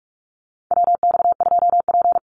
Morse code, heard as a keyed tone, about 44 wpm, 700 Hz. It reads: WX2P
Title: USA Call Signs Repeated 3x 45wpm